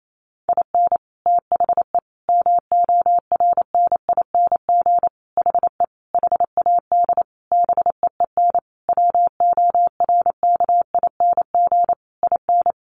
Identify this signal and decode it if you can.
Morse code, heard as a keyed tone, about 28 wpm, 700 Hz. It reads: IN THE MORNING HE HAD BEEN WORKING IN
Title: Sentences from Top 1200 Words 28wpm